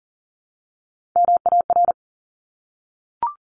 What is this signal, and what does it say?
Morse code, heard as a keyed tone, about 40 wpm, 700 Hz. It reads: MAR
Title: ARRL Sections 40wpm